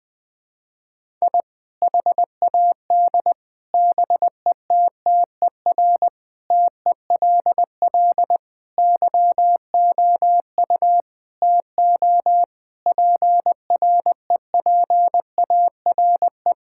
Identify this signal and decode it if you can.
Morse code, heard as a keyed tone, about 21 wpm, 700 Hz. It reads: I HAD BETTER TELL YOU TO PREPARE
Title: Sentences from Top 1000 Words - Review Part 6 - 20wpm